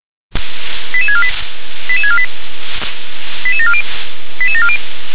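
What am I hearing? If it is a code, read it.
Five-tone selective call (ZVEI), heard as quick runs of five tones, tones of about 70 ms. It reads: 88648 88648 88740 88740
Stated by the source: ZVEI five-tone selective call received on an SDR.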